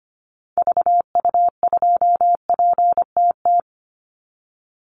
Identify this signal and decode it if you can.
Morse code, heard as a keyed tone, about 25 wpm, 700 Hz. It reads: VU2PTT
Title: Call Signs Repeated 2x - 25wpm